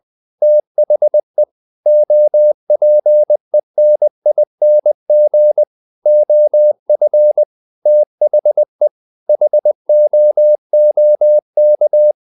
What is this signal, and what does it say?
Morse code, heard as a keyed tone, about 20 wpm, 600 Hz. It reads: THE OPENING OF THE HOOK
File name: the old man and the sea part 2 20wpm